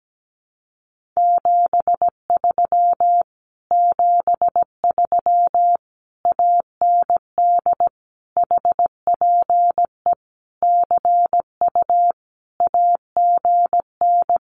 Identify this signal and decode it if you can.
Morse code, heard as a keyed tone, about 17 wpm, 700 Hz. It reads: 73 73 AND HPE CU AGN
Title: Common QSO Phrases Encore 17wpm